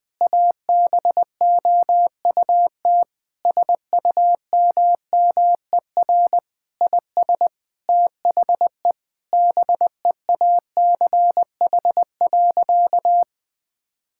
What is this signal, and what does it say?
Morse code, heard as a keyed tone, about 20 wpm, 700 Hz. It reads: ABOUT SUMMER IS THE BEACH.
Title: Sentences with 6-Word Phrases - 20wpm